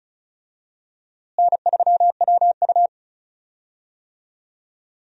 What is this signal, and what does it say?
Morse code, heard as a keyed tone, about 36 wpm, 700 Hz. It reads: N3WU
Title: USA Call Signs - 4 Characters - Repeated 2x - 35wpm